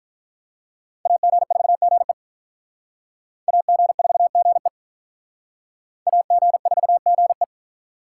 Morse code, heard as a keyed tone, about 44 wpm, 700 Hz. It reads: AG4GE AG4GE AG4GE
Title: Speed Racing - USA 2x2 Call Signs 35wpm